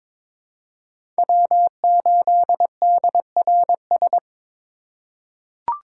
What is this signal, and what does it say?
Morse code, heard as a keyed tone, about 22 wpm, 700 Hz. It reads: W8DRS
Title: USA Call Signs - 5 Characters - 3x Character Spacing - 22wpm